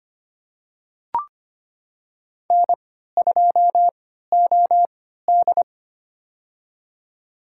Morse code, heard as a keyed tone, about 25 wpm, spaced as for 14 wpm, 700 Hz. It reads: N2OD
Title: USA Call Signs - 3x Character Spacing - 25wpm